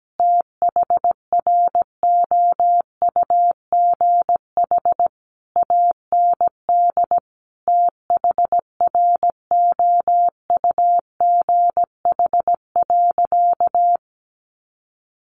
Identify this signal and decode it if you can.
Morse code, heard as a keyed tone, about 17 wpm, 700 Hz. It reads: THROUGH AND THROUGH.
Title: Sentences from Top 1400 Words 17wpm